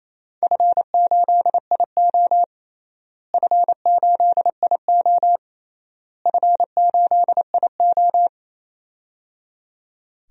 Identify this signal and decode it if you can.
Morse code, heard as a keyed tone, about 28 wpm, 700 Hz. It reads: F8IO F8IO F8IO
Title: DXCC-100 Call Signs - 4 Characters - Repeated 3x - 28wpm